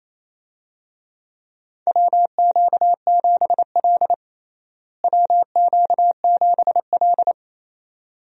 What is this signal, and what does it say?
Morse code, heard as a keyed tone, about 28 wpm, 700 Hz. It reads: WQ7L WQ7L
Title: USA Call Signs - 4 Characters - Repeated 2x - 28wpm